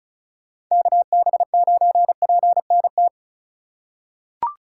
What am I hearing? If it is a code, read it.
Morse code, heard as a keyed tone, about 36 wpm, 700 Hz. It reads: KB9PNT
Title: USA Call Signs - 6 Characters - 4x Character Spacing - 35wpm